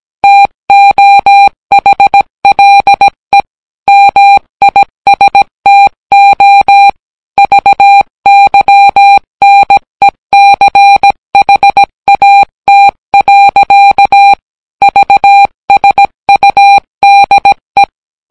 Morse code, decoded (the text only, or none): TOHLE MISTO VYNECHAT. VSUDE